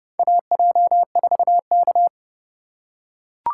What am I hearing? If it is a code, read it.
Morse code, heard as a keyed tone, about 30 wpm, 700 Hz. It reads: AJ4K
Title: USA Call Signs - 4 Characters - 3x Character Spacing - 30wpm